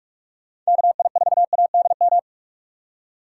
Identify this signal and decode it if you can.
Morse code, heard as a keyed tone, about 42 wpm, 700 Hz. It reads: KI4ADM